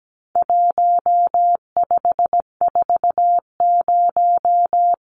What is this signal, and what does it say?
Morse code, heard as a keyed tone, about 17 wpm, 700 Hz. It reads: 1540
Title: Numbers - 4-Digit 17wpm